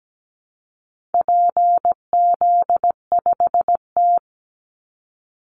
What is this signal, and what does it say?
Morse code, heard as a keyed tone, about 17 wpm, 700 Hz. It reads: PZ5T
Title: Call Signs Repeated 2x - 17wpm